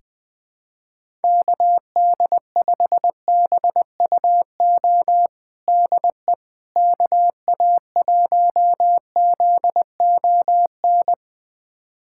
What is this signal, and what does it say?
Morse code, heard as a keyed tone, about 20 wpm, 700 Hz. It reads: KD5BUO DE KA1ZON
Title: USA Callsign DE Callsign - 6 Characters - 20wpm